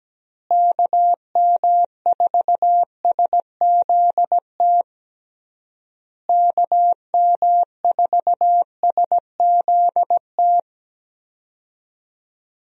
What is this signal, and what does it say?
Morse code, heard as a keyed tone, about 17 wpm, 700 Hz. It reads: KM4SZT KM4SZT
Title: USA Call Signs - 6 Characters - Repeated 3x - 17wpm